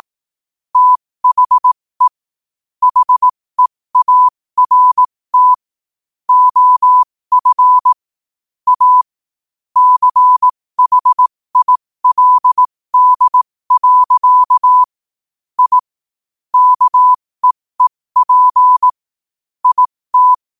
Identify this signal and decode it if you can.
Morse code, heard as a keyed tone, about 18 wpm, spaced as for 15 wpm, 1000 Hz. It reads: THE HEART OF A CHILD. I KEEP IT
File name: qotdfiles / 15wpm